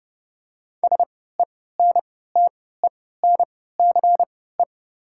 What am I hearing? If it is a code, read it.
Morse code, heard as a keyed tone, about 30 wpm, spaced as for 17 wpm, 700 Hz. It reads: SENTENCE
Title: 3x Character Spacing - Top 300 Words - 30wpm